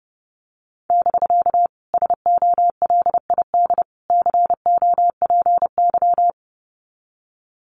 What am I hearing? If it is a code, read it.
Morse code, heard as a keyed tone, about 30 wpm, 700 Hz. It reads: <BK> SOLID COPY
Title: Common QSO Phrases Encore 30wpm